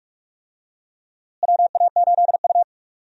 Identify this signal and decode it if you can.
Morse code, heard as a keyed tone, about 46 wpm, 700 Hz. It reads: WA8U
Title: USA Call Signs - 4 Characters - Repeated 2x - 45wpm